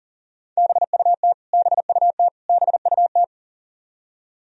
Morse code, heard as a keyed tone, about 40 wpm, 700 Hz. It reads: BUT BUT BUT
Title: Top 100 Words Repeated 40wpm